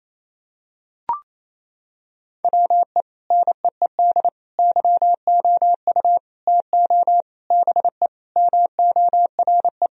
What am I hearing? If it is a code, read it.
Morse code, heard as a keyed tone, about 28 wpm, 700 Hz. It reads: WE NEED YOU TO BE MORE
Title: Sentences from Top 1000 Words - Review Part 3 - 28wpm